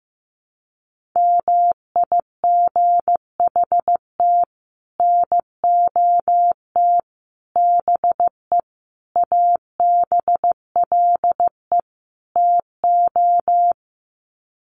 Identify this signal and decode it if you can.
Morse code, heard as a keyed tone, about 15 wpm, 700 Hz. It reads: MIGHT NOT BE ABLE TO
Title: Sets of 5 Words Encore2 15wpm